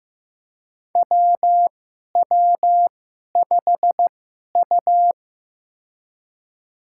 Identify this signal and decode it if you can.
Morse code, heard as a keyed tone, about 15 wpm, spaced as for 11 wpm, 700 Hz. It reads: WW5U
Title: USA Call Signs - 4 Characters - 2x Character Spacing - 15wpm